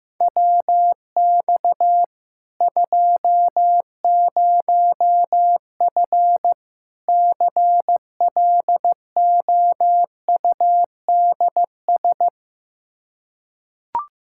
Morse code, heard as a keyed tone, about 16 wpm, 700 Hz. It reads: WX 20F CLOUDS
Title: Practice QSO - Exchange 2 - Info - 15wpm